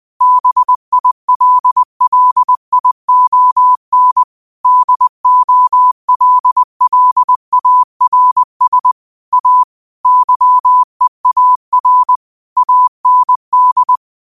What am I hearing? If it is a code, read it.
Morse code, heard as a keyed tone, about 20 wpm, 1000 Hz. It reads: BILLION DOLLARS A YEAR AND
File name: qotdfiles / 20wpm